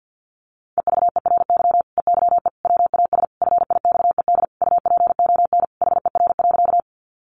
Morse code, heard as a keyed tone, about 50 wpm, 700 Hz. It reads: EVERY EYE WAS FIXED UPON HER.
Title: Sentences from Top 1100 Words 50wpm